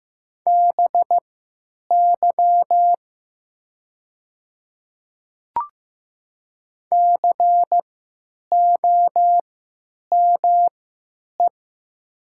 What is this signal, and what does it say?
Morse code, heard as a keyed tone, about 15 wpm, spaced as for 9 wpm, 700 Hz. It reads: BY COME
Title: 3x Character Spacing - Top 200 Words - 15wpm